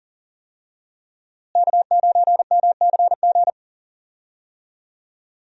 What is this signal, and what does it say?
Morse code, heard as a keyed tone, about 40 wpm, 700 Hz. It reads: K9MCG
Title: USA Call Signs - 5 Characters - Repeated 2x - 40wpm